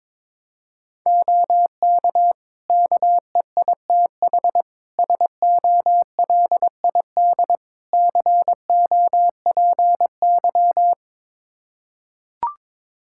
Morse code, heard as a keyed tone, about 22 wpm, 700 Hz. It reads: OK KEITH SOLID COPY
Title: Practice QSO - Exchange 3 - More Info - 22wpm